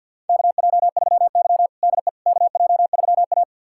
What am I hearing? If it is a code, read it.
Morse code, heard as a keyed tone, about 52 wpm, 700 Hz. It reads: KJ3Y DE KJ3A
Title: USA Callsign DE Callsign - 4 Characters - 50wpm